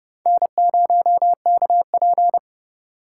Morse code, heard as a keyed tone, about 30 wpm, 700 Hz. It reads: N0KP